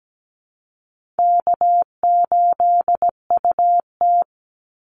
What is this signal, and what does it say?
Morse code, heard as a keyed tone, about 17 wpm, 700 Hz. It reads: K8UT
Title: Call Signs Repeated 3x - 17wpm